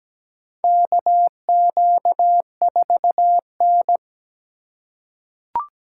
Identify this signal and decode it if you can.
Morse code, heard as a keyed tone, about 17 wpm, 700 Hz. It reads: KQ4N
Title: USA Call Signs - 3x Character Spacing - 17wpm